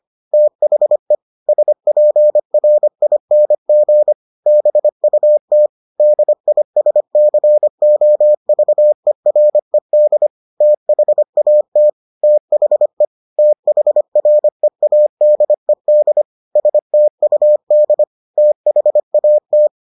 Morse code, heard as a keyed tone, about 25 wpm, 600 Hz. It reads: THE SPRING BUT DISCOVERED THAT THE THREADED STUD THAT